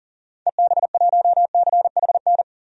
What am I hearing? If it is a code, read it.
Morse code, heard as a keyed tone, about 40 wpm, 700 Hz. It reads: EB1CHN
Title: DXCC-100 Call Signs - 6 Characters - 40wpm